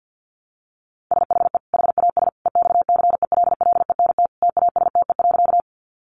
Morse code, heard as a keyed tone, about 48 wpm, 700 Hz. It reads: SHE HAS EXCELLENT TASTE.
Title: Sentences from Top 1100 Words 50wpm